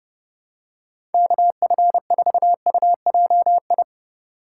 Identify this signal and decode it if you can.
Morse code, heard as a keyed tone, about 30 wpm, 700 Hz. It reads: KF4UJI